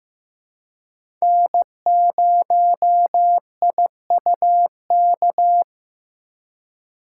Morse code, heard as a keyed tone, about 15 wpm, 700 Hz. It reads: N0IUK
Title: USA Call Signs - 5 Characters - 4x Character Spacing - 15wpm